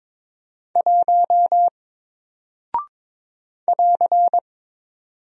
Morse code, heard as a keyed tone, about 22 wpm, 700 Hz. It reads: 1 <AR>
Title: Single Character (Letter:Number:Punctuation:Prosign) Alt 22wpm